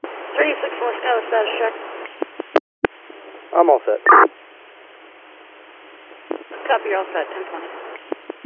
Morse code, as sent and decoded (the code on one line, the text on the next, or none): none